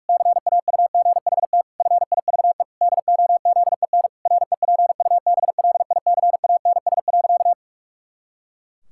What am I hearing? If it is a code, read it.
Morse code, heard as a keyed tone, about 46 wpm, 700 Hz. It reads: CAUGHT FIVE DOZEN REPUBLICANS.
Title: Pangram Challenge v2 - 45wpm